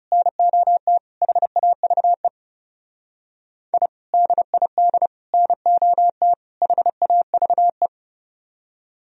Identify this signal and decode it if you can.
Morse code, heard as a keyed tone, about 34 wpm, 700 Hz. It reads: NOT HAVE I DID NOT HAVE
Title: Speed-Racing - Sets of 4 Words 30wpm